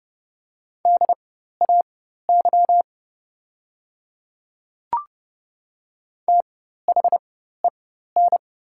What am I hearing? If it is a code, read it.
Morse code, heard as a keyed tone, about 30 wpm, spaced as for 14 wpm, 700 Hz. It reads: DAY THEN
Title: 4x Character Spacing - Top 100 Words - 30wpm